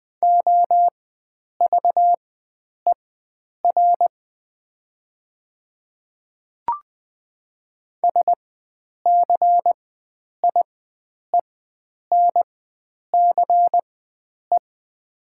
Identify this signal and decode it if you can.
Morse code, heard as a keyed tone, about 20 wpm, spaced as for 9 wpm, 700 Hz. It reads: OVER SCIENCE
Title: 4x Character Spacing - Top 500 Words - 20wpm